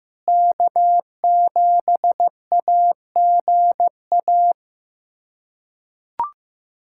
Morse code, heard as a keyed tone, about 15 wpm, 700 Hz. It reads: K7AGA
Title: USA Callsign DE Callsign 15wpm